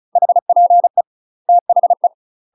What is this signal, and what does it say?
Morse code, heard as a keyed tone, about 34 wpm, 700 Hz. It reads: HPE THE
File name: UK MASTER QSOs TRAINING / 35WPM / UK MASTER QSOs 0002